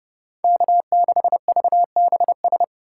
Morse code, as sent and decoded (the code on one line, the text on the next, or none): -.- -.... ...- -... ...
K6VBS